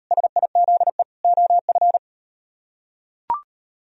Morse code, heard as a keyed tone, about 37 wpm, 700 Hz. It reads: SIZE OF